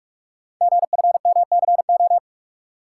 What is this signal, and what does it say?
Morse code, heard as a keyed tone, about 44 wpm, 690 Hz. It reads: GFMCO
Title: Code Groups - 5 Letters - 45wpm